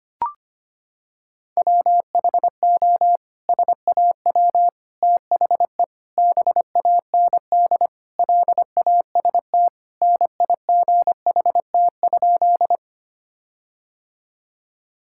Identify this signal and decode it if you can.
Morse code, heard as a keyed tone, about 25 wpm, 700 Hz. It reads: WHO SAW THE BAND LAST NIGHT?
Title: Sentences from Top 1600 Words 25wpm